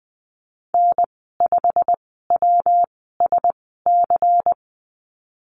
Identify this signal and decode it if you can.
Morse code, heard as a keyed tone, about 20 wpm, spaced as for 15 wpm, 700 Hz. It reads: N5WSC
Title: 154 Lesson 25g - Letter p - Call Signs 20wpm